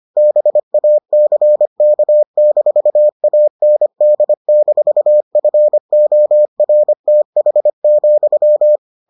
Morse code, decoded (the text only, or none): BACK-AND-FORTH,